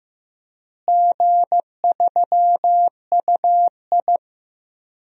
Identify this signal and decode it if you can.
Morse code, heard as a keyed tone, about 15 wpm, 700 Hz. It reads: G3UI
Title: DXCC-100 Call Signs - 4 Characters - Repeated 2x - 15wpm